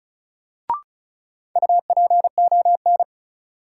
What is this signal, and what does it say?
Morse code, heard as a keyed tone, about 36 wpm, 700 Hz. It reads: UPON
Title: Sentences from Top 1100 Words - New Word Drill Alt - 35wpm